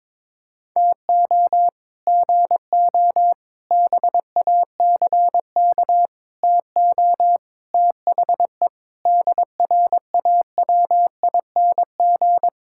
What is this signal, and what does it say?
Morse code, heard as a keyed tone, about 22 wpm, 700 Hz. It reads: TO GO BACK TO THE DRAWING
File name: practice / 22wpm / Sets of 7 Words - 22wpm